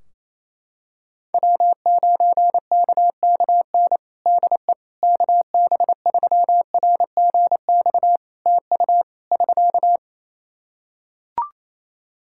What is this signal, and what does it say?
Morse code, heard as a keyed tone, about 28 wpm, 700 Hz. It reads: W9KKN DE KB3RGX TU <SK>
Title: Practice QSO - Exchange 4 - Ending - 28wpm